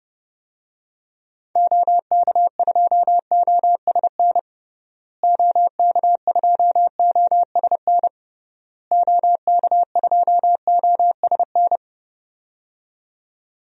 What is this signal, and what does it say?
Morse code, heard as a keyed tone, about 30 wpm, 700 Hz. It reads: OK2OSN OK2OSN OK2OSN
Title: DXCC-100 Call Signs - 6 Characters - Repeated 3x - 30wpm